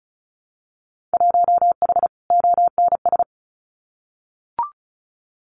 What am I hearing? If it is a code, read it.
Morse code, heard as a keyed tone, about 36 wpm, 700 Hz. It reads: 1H ONS
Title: Winter Field Day Exchanges 35wpm